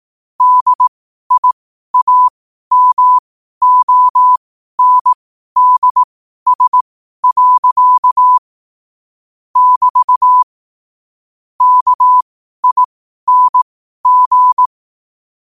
Morse code, heard as a keyed tone, about 18 wpm, spaced as for 12 wpm, 1000 Hz. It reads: DIAMONDS. <BT> KING
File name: qotdfiles / 12wpm